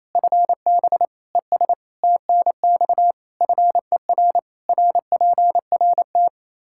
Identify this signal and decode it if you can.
Morse code, heard as a keyed tone, about 28 wpm, 700 Hz. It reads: FB ES TNX FER RPRT
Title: Practice QSO - Exchange 2 - Info - 28wpm